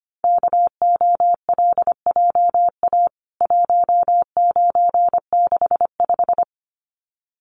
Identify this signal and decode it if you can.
Morse code, heard as a keyed tone, about 25 wpm, 700 Hz. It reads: KOLJA 1965
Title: CWT Exchange 25wpm